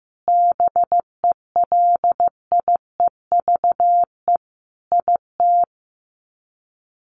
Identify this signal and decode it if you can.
Morse code, heard as a keyed tone, about 15 wpm, 700 Hz. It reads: BELIEVE IT
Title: Sets of 5 Words Encore 15wpm